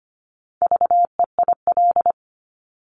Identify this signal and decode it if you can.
Morse code, heard as a keyed tone, about 25 wpm, 700 Hz. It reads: VEIL
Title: Multi-letter Phonemes Repeated 3x 25wpm